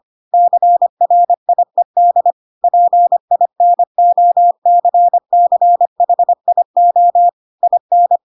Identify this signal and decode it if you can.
Morse code, heard as a keyed tone, about 25 wpm, 700 Hz. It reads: CRIED PINOCCHIO IN
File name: Pinocchio / Pinocchio ch7 25 25 ik1hge